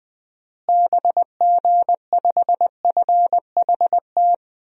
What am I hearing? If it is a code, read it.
Morse code, heard as a keyed tone, about 21 wpm, 700 Hz. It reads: BG5FHT
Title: DXCC-100 Call Signs - 6 Characters - Repeated 3x - 20wpm